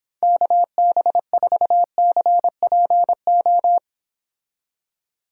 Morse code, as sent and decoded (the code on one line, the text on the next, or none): -.- -... ....- -.-. .--. ---
KB4CPO